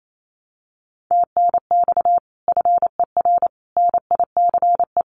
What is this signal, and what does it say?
Morse code, heard as a keyed tone, about 28 wpm, 700 Hz. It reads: TNX FER NICE
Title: Common QSO Phrases 28wpm